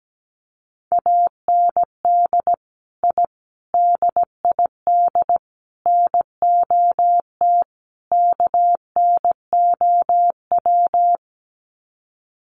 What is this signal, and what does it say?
Morse code, heard as a keyed tone, about 17 wpm, 700 Hz. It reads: AND I DID NOT KNOW
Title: Sets of 5 Words Encore 17wpm